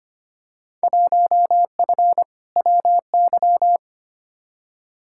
Morse code, heard as a keyed tone, about 25 wpm, 700 Hz. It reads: 1F WY
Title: Field Day Exchanges Encore - 25wpm